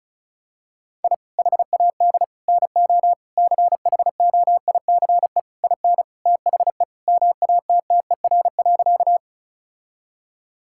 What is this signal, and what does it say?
Morse code, heard as a keyed tone, about 36 wpm, 700 Hz. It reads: I HAD NO CHOICE IN THE MATTER.
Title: Sentences from Top 1000 Words - Review Part 6 - 35wpm